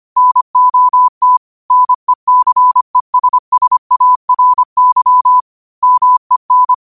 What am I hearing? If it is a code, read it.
Morse code, heard as a keyed tone, about 25 wpm, 1000 Hz. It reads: NOT NECESSARY MEN
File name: qotdfiles / 25wpm